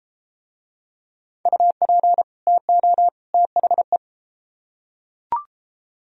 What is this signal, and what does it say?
Morse code, heard as a keyed tone, about 31 wpm, 700 Hz. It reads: UP TO THE